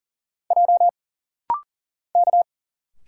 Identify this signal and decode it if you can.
Morse code, heard as a keyed tone, about 40 wpm, 700 Hz. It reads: J K
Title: Single Letters - Rapid-Fire 40wpm